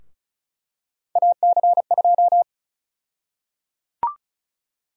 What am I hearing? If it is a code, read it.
Morse code, heard as a keyed tone, about 36 wpm, 700 Hz. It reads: AC2
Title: USA Call Sign Prefixes - 35wpm